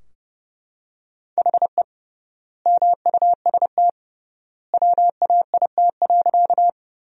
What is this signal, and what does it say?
Morse code, heard as a keyed tone, about 32 wpm, 700 Hz. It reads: HE MUST WAIT.
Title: 3x Word-Spacing - Sentences from Top 500 Words - 30wpm